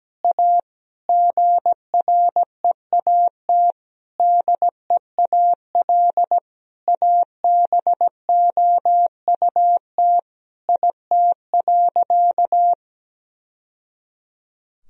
Morse code, heard as a keyed tone, about 17 wpm, 700 Hz. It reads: A GREAT DEAL ABOUT IT.
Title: Sentences from Top 1000 Words - Review Part 1 - 17wpm